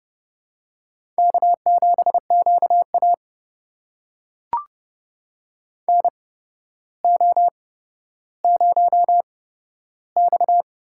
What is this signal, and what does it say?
Morse code, heard as a keyed tone, about 32 wpm, spaced as for 13 wpm, 700 Hz. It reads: K7QA NO0X
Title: USA Call Signs - 4 Characters - 8x Character Spacing - 30wpm